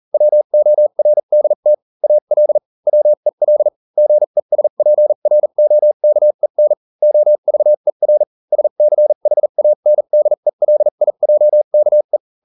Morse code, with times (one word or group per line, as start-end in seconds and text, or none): 0.1-1.7 WORDT
2.0-2.6 AL
2.9-3.7 WEL
4.0-6.7 GESPROKEN
7.0-8.2 OVER
8.5-12.2 SCHANDELIJKE